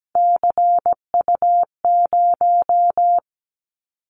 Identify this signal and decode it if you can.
Morse code, heard as a keyed tone, about 17 wpm, 700 Hz. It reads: CU0
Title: Callsign Prefixes for Top 50 DXCC - 17wpm